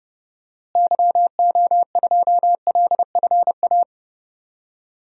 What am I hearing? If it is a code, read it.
Morse code, heard as a keyed tone, about 30 wpm, 700 Hz. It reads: YO2LFA
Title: DXCC-100 Call Signs - 6 Characters - Repeated 2x - 30wpm